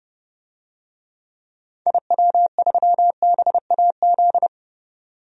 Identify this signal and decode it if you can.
Morse code, heard as a keyed tone, about 30 wpm, 700 Hz. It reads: IW3BAZ
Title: DXCC-100 Call Signs - 6 Characters - 30wpm